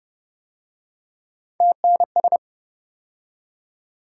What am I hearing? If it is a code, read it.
Morse code, heard as a keyed tone, about 31 wpm, 700 Hz. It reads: TNS